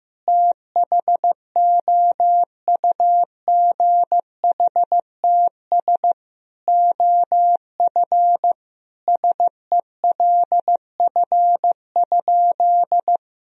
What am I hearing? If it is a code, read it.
Morse code, heard as a keyed tone, about 15 wpm, 700 Hz. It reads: THOUGHTS OF SELF?
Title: Sentences from Top 1000 Words - Review Part 1 - 15wpm